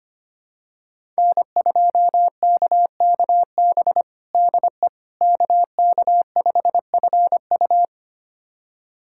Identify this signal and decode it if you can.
Morse code, heard as a keyed tone, about 25 wpm, 700 Hz. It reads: N2KKB DE KK5FU
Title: USA Callsign DE Callsign 25wpm